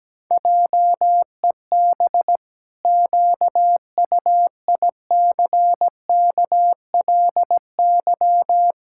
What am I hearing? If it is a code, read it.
Morse code, heard as a keyed tone, about 17 wpm, 700 Hz. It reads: JEB QUICKLY
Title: Pangram Challenge v2 - 17wpm